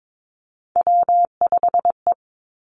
Morse code, heard as a keyed tone, about 22 wpm, 700 Hz. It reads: W5E
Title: Three Challenging Letter-Number Combinations 22wpm